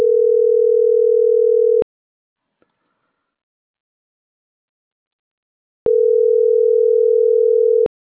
Ringback tone (North America), 440 + 480 Hz.